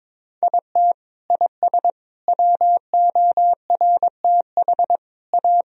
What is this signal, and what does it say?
Morse code, heard as a keyed tone, about 22 wpm, 700 Hz. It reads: IT IS WORTH A
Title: Sentences from Top 1000 Words - Review Part 3 - 22wpm